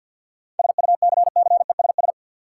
Morse code, heard as a keyed tone, about 52 wpm, 700 Hz. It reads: SUCCESS
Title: Seven Letter Words - 50wpm